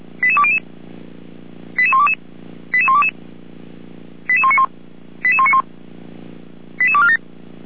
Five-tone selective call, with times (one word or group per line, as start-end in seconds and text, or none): 0.2-0.6 88200
1.8-2.1 88122
2.7-3.1 88122
4.3-4.7 88191
5.2-5.6 88191
6.8-7.2 88247